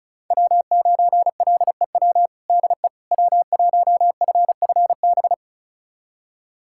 Morse code, heard as a keyed tone, about 34 wpm, 700 Hz. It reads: W9LEW DE W1FFB
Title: USA Callsign DE Callsign - 5 Characters - 35wpm